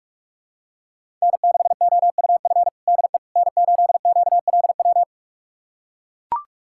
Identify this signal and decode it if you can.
Morse code, heard as a keyed tone, about 42 wpm, 700 Hz. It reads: N6OUF DE N8QLW
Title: USA Callsign DE Callsign - 5 Characters - 45wpm